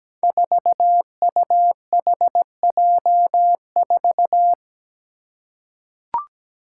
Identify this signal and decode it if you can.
Morse code, heard as a keyed tone, about 17 wpm, 700 Hz. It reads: 4UHJ4
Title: Five Challenging Letter-Number Combinations 17wpm